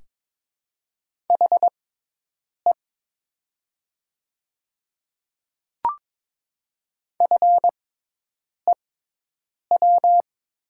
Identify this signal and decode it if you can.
Morse code, heard as a keyed tone, about 22 wpm, spaced as for 8 wpm, 700 Hz. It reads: HE FEW
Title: 6x Character Spacing - Top 300 Words - 22wpm